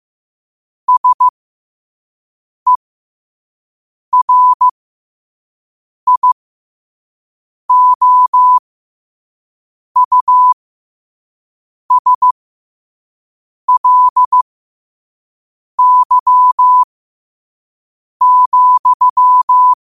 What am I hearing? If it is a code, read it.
Morse code, heard as a keyed tone, about 15 wpm, spaced as for 5 wpm, 1000 Hz. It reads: SERIOUSLY,
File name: qotdfiles / 5wpm